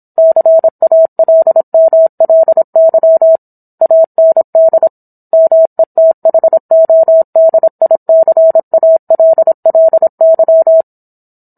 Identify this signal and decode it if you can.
Morse code, heard as a keyed tone, about 26 wpm, 650 Hz. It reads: CALMLY AND METHODICALLY